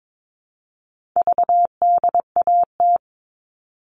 Morse code, heard as a keyed tone, about 22 wpm, 700 Hz. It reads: VDAT
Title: All Cut Numbers - 4-Digit 22wpm